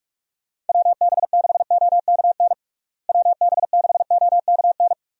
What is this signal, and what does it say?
Morse code, heard as a keyed tone, about 46 wpm, 700 Hz. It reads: WB6OKN WB6OKN
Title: USA Call Signs - 6 Characters - Repeated 3x - 45wpm